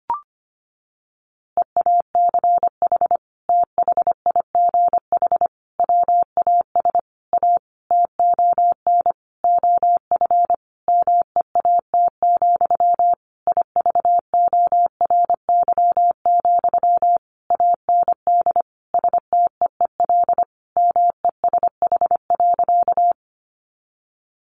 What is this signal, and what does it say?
Morse code, heard as a keyed tone, about 25 wpm, 700 Hz. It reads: EACH THIGH WAS A TON OF MEAT, IVORY, AND STEEL MESH.
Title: A Sound Of Thunder 25wpm